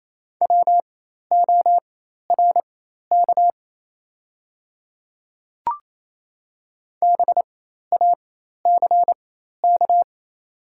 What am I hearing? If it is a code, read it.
Morse code, heard as a keyed tone, about 28 wpm, spaced as for 13 wpm, 700 Hz. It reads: WORK BACK
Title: 4x Character Spacing - Top 200 Words - 28wpm